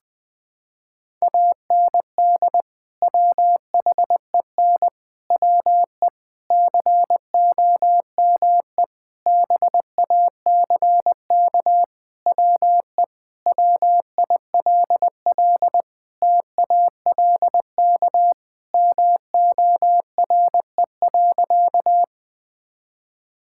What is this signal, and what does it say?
Morse code, heard as a keyed tone, about 20 wpm, 700 Hz. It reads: AND WHEN WE COME BACK WE WILL TALK MORE.
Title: Sentences with 5-Word Phrases - 20wpm